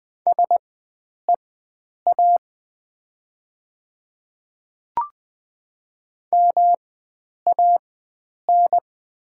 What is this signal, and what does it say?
Morse code, heard as a keyed tone, about 20 wpm, spaced as for 9 wpm, 700 Hz. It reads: SEA MAN
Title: 4x Character Spacing - Top 500 Words - 20wpm